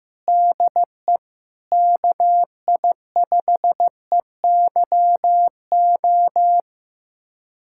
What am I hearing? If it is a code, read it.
Morse code, heard as a keyed tone, about 16 wpm, 700 Hz. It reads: DE KI5EYO
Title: USA Callsign DE Callsign - 6 Characters - 15wpm